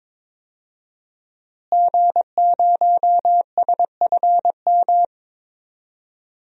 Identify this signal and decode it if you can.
Morse code, heard as a keyed tone, about 22 wpm, 700 Hz. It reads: G0SFM
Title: DXCC-100 Call Signs - 5 Characters - Repeated 2x - 22wpm